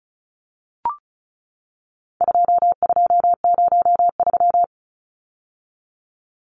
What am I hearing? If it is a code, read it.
Morse code, heard as a keyed tone, about 34 wpm, 700 Hz. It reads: 2203
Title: Numbers - 4-Digit 35wpm